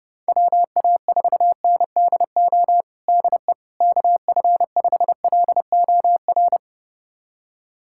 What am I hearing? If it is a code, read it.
Morse code, heard as a keyed tone, about 30 wpm, 700 Hz. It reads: WA4NDO DE KF5LOR
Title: USA Callsign DE Callsign 30wpm